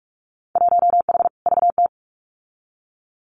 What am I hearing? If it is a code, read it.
Morse code, heard as a keyed tone, about 44 wpm, 700 Hz. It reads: 1H VT
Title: Winter Field Day Exchanges Encore 45wpm